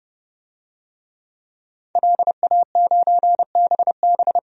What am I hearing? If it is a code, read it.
Morse code, heard as a keyed tone, about 32 wpm, 700 Hz. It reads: LA9BB
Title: DXCC-100 Call Signs - 5 Characters - 30wpm